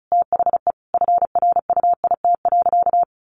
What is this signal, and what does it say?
Morse code, heard as a keyed tone, about 34 wpm, 700 Hz. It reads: THE FRUIT.
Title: Sentences from Top 1300 Words 35wpm